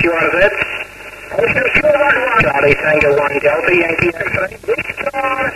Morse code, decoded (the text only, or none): none